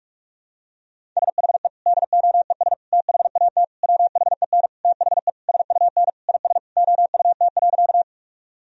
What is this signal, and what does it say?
Morse code, heard as a keyed tone, about 47 wpm, 700 Hz. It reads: SHE DOES THAT WHEN THE SUN IS OUT.
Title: Sentences from Top 1000 Words - Review Part 1 - 45wpm